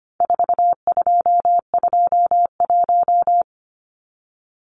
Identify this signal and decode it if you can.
Morse code, heard as a keyed tone, about 25 wpm, 700 Hz. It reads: 4221
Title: Numbers - 4-Digit 25wpm